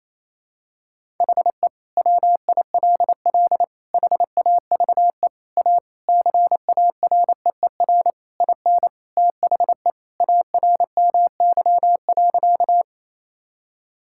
Morse code, heard as a keyed tone, about 28 wpm, 700 Hz. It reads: HE WILL HAVE A CAREER IN THE ARMY.
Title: Sentences from Top 1000 Words - Review Part 6 - 28wpm